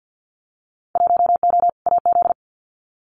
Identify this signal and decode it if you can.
Morse code, heard as a keyed tone, about 50 wpm, 700 Hz. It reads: 1O AZ
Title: Winter Field Day Exchanges 50wpm